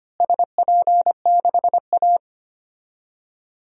Morse code, heard as a keyed tone, about 25 wpm, 700 Hz. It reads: SP6A
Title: DXCC-100 Call Signs - 4 Characters - Repeated 2x - 25wpm